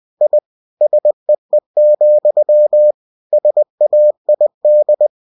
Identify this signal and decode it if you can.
Morse code, heard as a keyed tone, about 20 wpm, 600 Hz. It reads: I SEE, SAID